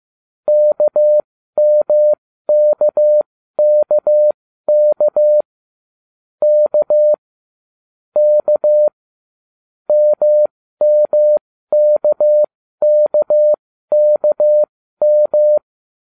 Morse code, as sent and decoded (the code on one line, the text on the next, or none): -.- -- -.- -.- -.- / -.- / -.- / -- -- -.- -.- -.- --
KMKKK K K MMKKKM